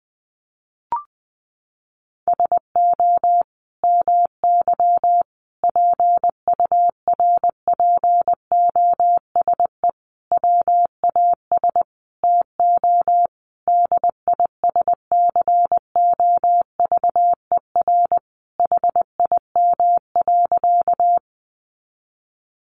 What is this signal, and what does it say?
Morse code, heard as a keyed tone, about 20 wpm, 700 Hz. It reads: SO MY PURPOSE WAS TO DISCOVER HIM.
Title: Sentences from Top 1100 Words 20wpm